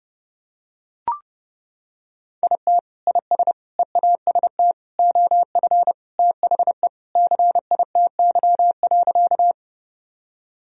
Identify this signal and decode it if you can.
Morse code, heard as a keyed tone, about 32 wpm, 700 Hz. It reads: IT IS EAST OF THE CITY.
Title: Sentences from Top 1000 Words - Review Part 3 - 30wpm